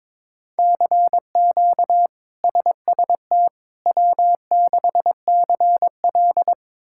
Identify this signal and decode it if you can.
Morse code, heard as a keyed tone, about 21 wpm, 700 Hz. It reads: CQ SST W6CL
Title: K1USN SST QSO Practice - 22wpm